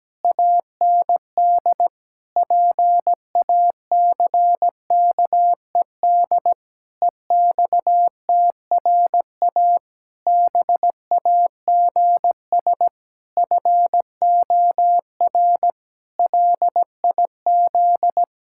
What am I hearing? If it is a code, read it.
Morse code, heard as a keyed tone, about 17 wpm, 700 Hz. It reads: AND PACKED EXTRA BAGS FOR LIZ
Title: Pangram Challenge v2 - 17wpm